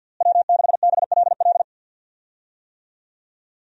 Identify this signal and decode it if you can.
Morse code, heard as a keyed tone, about 49 wpm, 700 Hz. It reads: W6BLL
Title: USA Call Signs - 5 Characters - Repeated 2x - 50wpm